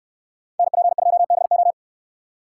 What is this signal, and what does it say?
Morse code, heard as a keyed tone, about 69 wpm, 700 Hz. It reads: NP2BO